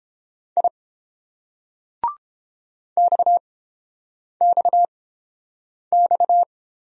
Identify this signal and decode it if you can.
Morse code, heard as a keyed tone, about 30 wpm, spaced as for 8 wpm, 700 Hz. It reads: I XXX